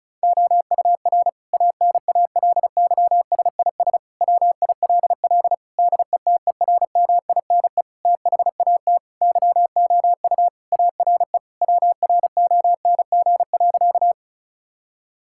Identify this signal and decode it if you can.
Morse code, heard as a keyed tone, about 34 wpm, 700 Hz. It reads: OUR ANALYSIS WILL DETERMINE THAT YOU ARE WRONG.
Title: Sentences from Top 1000 Words - Review Part 5 - 35wpm